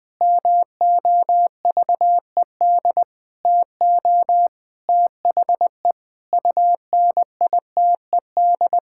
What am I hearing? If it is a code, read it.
Morse code, heard as a keyed tone, about 20 wpm, 700 Hz. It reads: MOVED TO THE UNITED
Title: Sentences with 4-Word Phrases - 20wpm